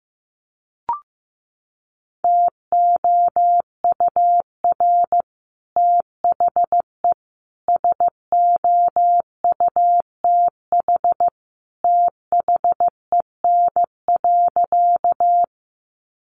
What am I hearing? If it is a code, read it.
Morse code, heard as a keyed tone, about 15 wpm, 700 Hz. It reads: TOUR THE SOUTH THEN.
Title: Sentences from Top 1400 Words 15wpm